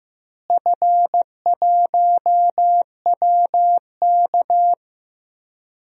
Morse code, heard as a keyed tone, about 16 wpm, 700 Hz. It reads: F1WK
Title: DXCC-100 Call Signs - 4 Characters - Repeated 3x - 15wpm